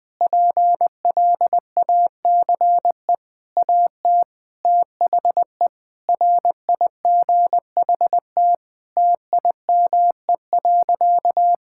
Morse code, heard as a keyed tone, about 20 wpm, 700 Hz. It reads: PLACE AT THE RIGHT TIME.
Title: Sentences with 9-Word Phrases - 20wpm